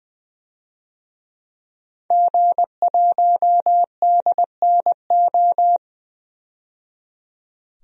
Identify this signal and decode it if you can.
Morse code, heard as a keyed tone, about 20 wpm, 700 Hz. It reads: G1DNO
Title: DXCC-100 Call Signs - 5 Characters - 20wpm